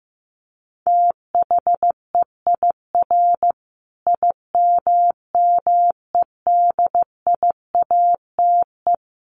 Morse code, heard as a keyed tone, about 15 wpm, 700 Hz. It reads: THEIR IMMEDIATE
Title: Sentences from Top 1200 Words 15wpm